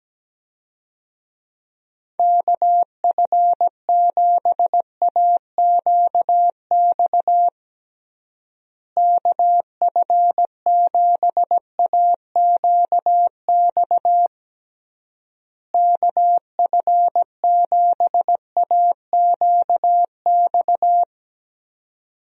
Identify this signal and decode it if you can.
Morse code, heard as a keyed tone, about 17 wpm, 700 Hz. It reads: KF7AQX KF7AQX KF7AQX
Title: USA Call Signs - 6 Characters - Repeated 3x - 17wpm